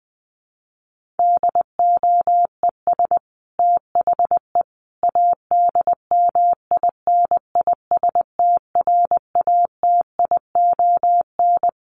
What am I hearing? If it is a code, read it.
Morse code, heard as a keyed tone, about 20 wpm, 700 Hz. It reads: DOES THE ADMINISTRATION
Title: Sentences from Top 1300 Words 20wpm